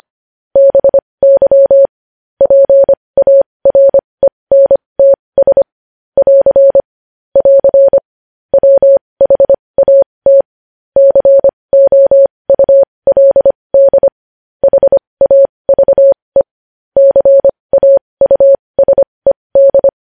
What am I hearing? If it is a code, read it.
Morse code, heard as a keyed tone, about 25 wpm, spaced as for 20 wpm, 550 Hz. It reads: BY PARENTS <AR> <AR> WHAT COULD HAVE CAUSED
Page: News headlines in Morse code at 20 wpm